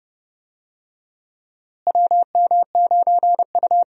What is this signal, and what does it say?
Morse code, heard as a keyed tone, about 30 wpm, 700 Hz. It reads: WM9U
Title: USA Call Signs - 4 Characters - Repeated 2x - 30wpm